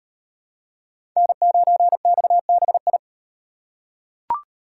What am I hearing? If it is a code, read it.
Morse code, heard as a keyed tone, about 38 wpm, 700 Hz. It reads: N9XBI